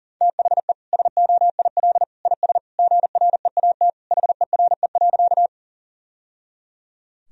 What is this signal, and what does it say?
Morse code, heard as a keyed tone, about 40 wpm, 700 Hz. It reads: THE SOIL IS GREAT HERE.
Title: Sentences from Top 1000 Words - Review Part 4 - 40wpm